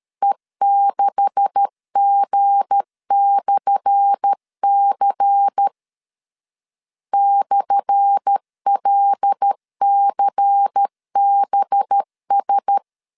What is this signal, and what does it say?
Morse code, heard as a keyed tone, about 13 wpm, 800 Hz. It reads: E6G/C /LCBS
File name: LEZIONI / 10 SIGNS / 10A SLASH 12wpm 2314